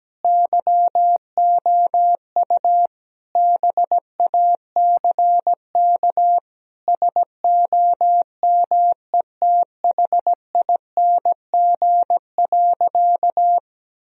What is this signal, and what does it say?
Morse code, heard as a keyed tone, about 17 wpm, 700 Hz. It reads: YOU BACK SOMETHING.
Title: Sentences from Top 1000 Words - Review Part 3 - 17wpm